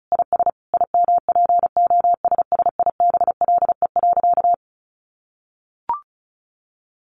Morse code, heard as a keyed tone, about 34 wpm, 700 Hz. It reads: IS IMPOSSIBLE.
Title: Sentences from Top 1100 Words 35wpm